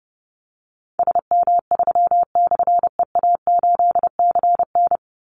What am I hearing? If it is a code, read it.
Morse code, heard as a keyed tone, about 30 wpm, 700 Hz. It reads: SM3/EA8CN
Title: Call Signs Repeated 2x - 30wpm